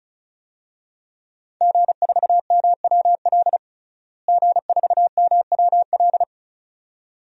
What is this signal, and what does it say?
Morse code, heard as a keyed tone, about 34 wpm, 700 Hz. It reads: G4MWL G4MWL
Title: DXCC-100 Call Signs - 5 Characters - Repeated 2x - 35wpm